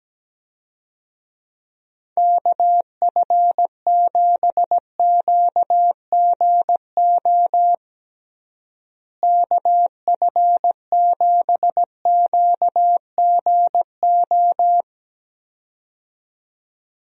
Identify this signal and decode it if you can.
Morse code, heard as a keyed tone, about 17 wpm, 700 Hz. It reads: KF7QGO KF7QGO
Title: USA Call Signs - 6 Characters - Repeated 2x - 17wpm